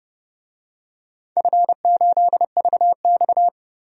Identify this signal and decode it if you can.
Morse code, heard as a keyed tone, about 30 wpm, 700 Hz. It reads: F8VX
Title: DXCC-100 Call Signs - 4 Characters - 30wpm